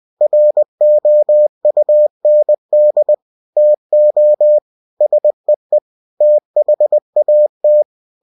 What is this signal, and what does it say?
Morse code, heard as a keyed tone, about 20 wpm, 600 Hz. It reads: ROUND TO SEE THAT
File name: cw-books / winnie-the-pooh / winnie-20wpm-page-0005